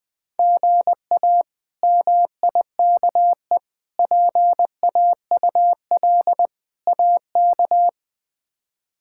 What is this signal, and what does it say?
Morse code, heard as a keyed tone, about 21 wpm, 700 Hz. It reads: GA MIKE PAUL AK
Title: K1USN SST QSO Practice - 20wpm